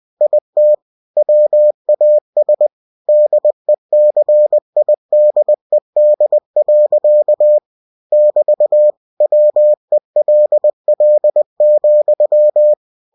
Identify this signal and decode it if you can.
Morse code, heard as a keyed tone, about 20 wpm, 600 Hz. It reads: IT WAS DECIDED. <BT> WELL,
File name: cw-books / winnie-the-pooh / winnie-20wpm-page-0006